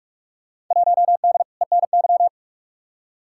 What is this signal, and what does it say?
Morse code, heard as a keyed tone, about 43 wpm, 700 Hz. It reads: 1D ENY
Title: Field Day Exchanges - 45wpm